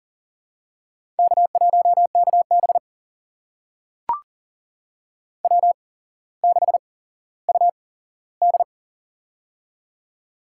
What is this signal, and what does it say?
Morse code, heard as a keyed tone, about 37 wpm, spaced as for 17 wpm, 700 Hz. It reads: K1KB W6UD
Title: USA Call Signs - 4 Characters - 8x Character Spacing - 40wpm